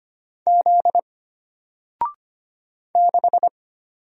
Morse code, heard as a keyed tone, about 25 wpm, 700 Hz. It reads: Z 6
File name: Single-Letter-Number-25wpm